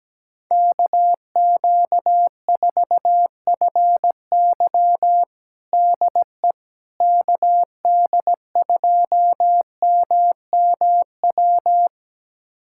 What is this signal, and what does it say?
Morse code, heard as a keyed tone, about 17 wpm, 700 Hz. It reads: KQ4FY DE KD2MMW
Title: USA Callsign DE Callsign 17wpm